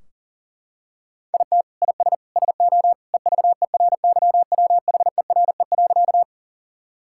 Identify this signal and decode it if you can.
Morse code, heard as a keyed tone, about 37 wpm, 700 Hz. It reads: IT IS SO EVERYWHERE.
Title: Sentences from Top 1600 Words 40wpm